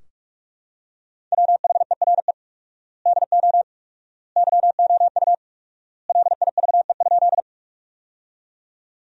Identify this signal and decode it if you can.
Morse code, heard as a keyed tone, about 42 wpm, 700 Hz. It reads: WHERE DO YOU LIVE?
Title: 4x Word-Spacing - Sentences from Top 200 Words - 45wpm